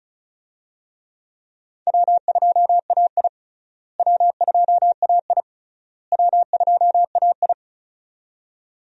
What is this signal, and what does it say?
Morse code, heard as a keyed tone, about 36 wpm, 700 Hz. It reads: W2AI W2AI W2AI
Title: USA Call Signs - 4 Characters - Repeated 3x - 35wpm